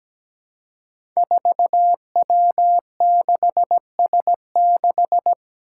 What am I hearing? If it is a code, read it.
Morse code, heard as a keyed tone, about 17 wpm, 700 Hz. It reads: 4W6S6
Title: Five Challenging Letter-Number Combinations 17wpm